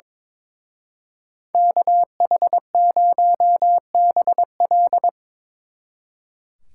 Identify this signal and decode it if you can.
Morse code, heard as a keyed tone, about 26 wpm, 700 Hz. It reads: KH0BL
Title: Speed Racing - USA 2x2 Call Signs 22wpm